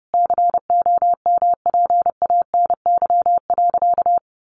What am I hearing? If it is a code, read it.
Morse code, heard as a keyed tone, about 30 wpm, 700 Hz. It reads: COMPANY.
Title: Sentences from Top 1100 Words 30wpm